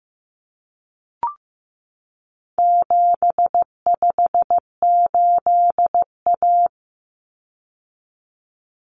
Morse code, heard as a keyed tone, about 15 wpm, 700 Hz. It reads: 758A
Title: Common Cut Numbers - 4-Digit 15wpm